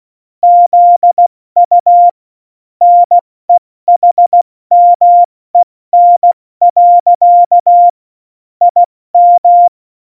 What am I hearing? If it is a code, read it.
Morse code, heard as a keyed tone, about 16 wpm, 700 Hz. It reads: ZU NEHMEN. IM